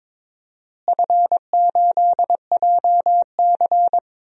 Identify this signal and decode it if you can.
Morse code, heard as a keyed tone, about 23 wpm, 700 Hz. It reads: F8JC
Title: DXCC-100 Call Signs - 4 Characters - Repeated 2x - 22wpm